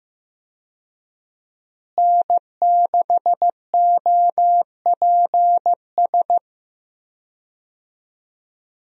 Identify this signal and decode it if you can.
Morse code, heard as a keyed tone, about 16 wpm, 700 Hz. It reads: N6OPS
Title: USA Call Signs - 5 Characters - Repeated 2x - 15wpm